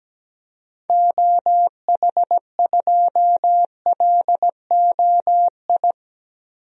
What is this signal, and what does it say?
Morse code, heard as a keyed tone, about 17 wpm, 700 Hz. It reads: OH2LOI
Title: DXCC-100 Call Signs - 6 Characters - Repeated 3x - 17wpm